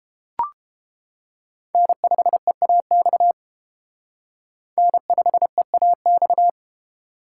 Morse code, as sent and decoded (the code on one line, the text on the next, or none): -. ..... .- -..- / -. ..... .- -..-
N5EAX N5EAX